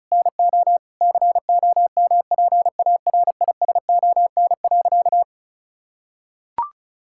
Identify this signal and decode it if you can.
Morse code, heard as a keyed tone, about 34 wpm, 700 Hz. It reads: NO COMPARISON.
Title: Sentences from Top 1600 Words 35wpm